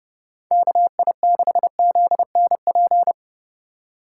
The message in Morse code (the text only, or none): KI6ZNP